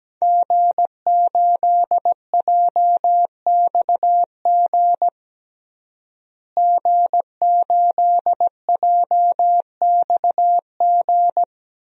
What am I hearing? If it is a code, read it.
Morse code, heard as a keyed tone, about 17 wpm, 700 Hz. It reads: G8JXG G8JXG
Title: DXCC-100 Call Signs - 5 Characters - Repeated 2x - 17wpm